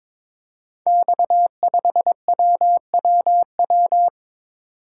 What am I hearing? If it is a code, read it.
Morse code, heard as a keyed tone, about 22 wpm, 700 Hz. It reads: X5WWW
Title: Code Groups - 5 Letter:Numbers - 22wpm